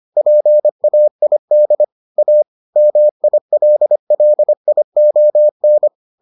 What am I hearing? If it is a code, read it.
Morse code, heard as a keyed tone, about 25 wpm, 600 Hz. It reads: PAID A MILLION